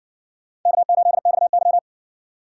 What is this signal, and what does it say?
Morse code, heard as a keyed tone, about 58 wpm, 700 Hz. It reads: K8XY